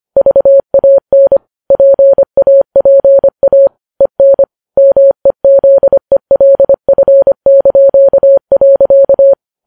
Morse code, heard as a keyed tone, about 25 wpm, 550 Hz. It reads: VAN PAPA EN MEZELF).